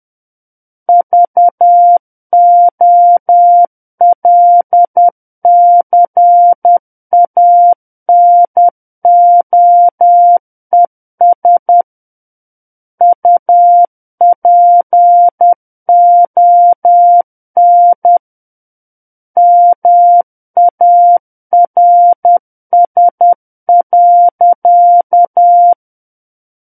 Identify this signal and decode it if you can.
Morse code, heard as a keyed tone, about 10 wpm, 700 Hz. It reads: VOLCANOES UPON MARS.
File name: Book One / 01-Chapter 1 - 10 WPM